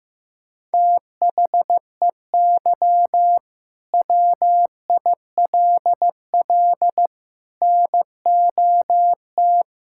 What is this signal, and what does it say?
Morse code, heard as a keyed tone, about 15 wpm, 700 Hz. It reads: THEY WILL NOT
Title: Sentences from Top 1000 Words - Review Part 5 - 15wpm